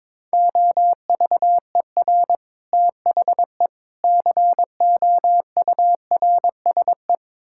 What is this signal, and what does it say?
Morse code, heard as a keyed tone, about 22 wpm, 700 Hz. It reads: OVER THE COURSE
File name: practice / 22wpm / Build A Phrase - 5-Word Phrases - 22wpm